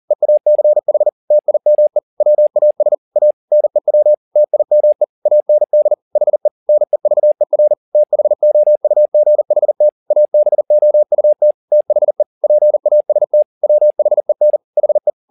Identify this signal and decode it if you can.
Morse code, heard as a keyed tone, about 40 wpm, 600 Hz. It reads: EACH TIME WAS A NEW TIME AND HE NEVER THOUGHT ABOUT THE PAST WHEN HE